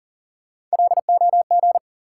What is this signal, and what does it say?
Morse code, heard as a keyed tone, about 40 wpm, 700 Hz. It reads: LOG
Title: Sentences from Top 1400 Words - New Word Drill Alt - 40wpm